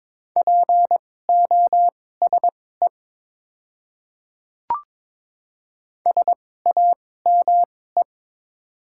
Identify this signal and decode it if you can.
Morse code, heard as a keyed tone, about 22 wpm, spaced as for 16 wpm, 700 Hz. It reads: POSE SAME
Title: 2x Character Spacing - Top 500 Words - 22wpm